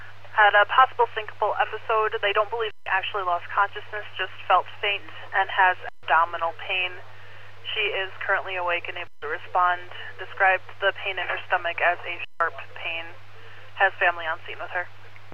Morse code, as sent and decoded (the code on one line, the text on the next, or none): none